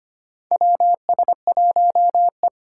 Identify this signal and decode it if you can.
Morse code, heard as a keyed tone, about 25 wpm, 700 Hz. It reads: WS1E
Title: USA Call Signs - 4 Characters - 4x Character Spacing - 25wpm